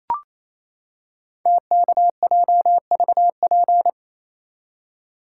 Morse code, heard as a keyed tone, about 28 wpm, 700 Hz. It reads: TKJVP
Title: Code Groups - 5 Letters - 28wpm